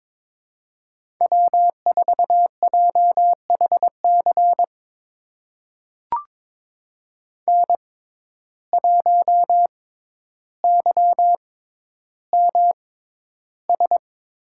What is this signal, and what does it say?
Morse code, heard as a keyed tone, about 22 wpm, spaced as for 11 wpm, 700 Hz. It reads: W4JHC N1YMS
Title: USA Call Signs - 5 Characters - 6x Character Spacing - 22wpm